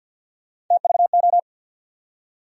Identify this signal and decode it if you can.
Morse code, heard as a keyed tone, about 47 wpm, 700 Hz. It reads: TVO
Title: USA Call Sign Postfixes - 3 Letters 50wpm